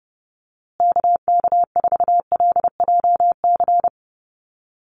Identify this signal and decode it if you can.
Morse code, heard as a keyed tone, about 30 wpm, 700 Hz. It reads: KK4LJC
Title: USA Call Signs Repeated 3x 30wpm